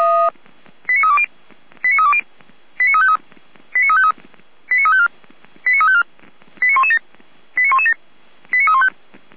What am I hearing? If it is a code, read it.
Five-tone selective call (ZVEI), heard as quick runs of five tones, tones of about 70 ms. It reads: 89320 89320 89353 89353 89354 89354 89117 89117 89315